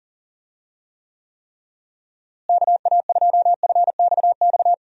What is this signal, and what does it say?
Morse code, heard as a keyed tone, about 37 wpm, 700 Hz. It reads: KA2FXX
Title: USA Call Signs - 6 Characters - Repeated 3x - 40wpm